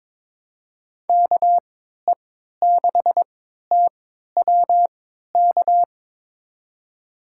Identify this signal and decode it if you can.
Morse code, heard as a keyed tone, about 22 wpm, spaced as for 12 wpm, 700 Hz. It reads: KE6TWK
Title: USA Call Signs - 6 Characters - 3x Character Spacing - 22wpm